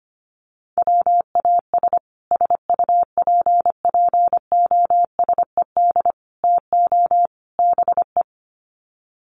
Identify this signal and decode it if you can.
Morse code, heard as a keyed tone, about 25 wpm, 700 Hz. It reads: WAS SUPPOSED TO BE
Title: Sets of 4 Words 25wpm